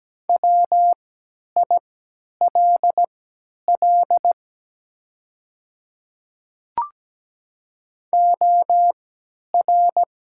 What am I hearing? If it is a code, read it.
Morse code, heard as a keyed tone, about 17 wpm, spaced as for 10 wpm, 690 Hz. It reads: WILL OR
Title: 3x Character Spacing - Top 100 Words - 17wpm